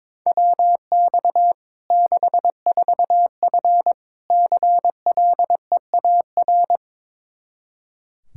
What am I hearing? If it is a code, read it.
Morse code, heard as a keyed tone, about 22 wpm, 700 Hz. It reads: WX 64F CLEAR
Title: Practice QSO - Exchange 2 - Info - 22wpm